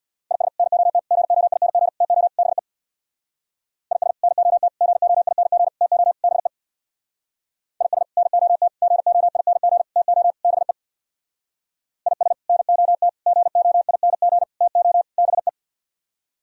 Morse code, heard as a keyed tone, about 64 wpm, 700 Hz. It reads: IS NOT GOING TO BE IS NOT GOING TO BE IS NOT GOING TO BE IS NOT GOING TO BE